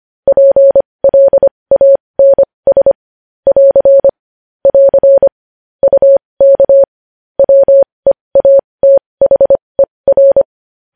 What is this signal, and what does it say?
Morse code, heard as a keyed tone, about 25 wpm, spaced as for 20 wpm, 550 Hz. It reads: PLANS <AR> <AR> UK WEATHER
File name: podcast / News headlines-1x-25x20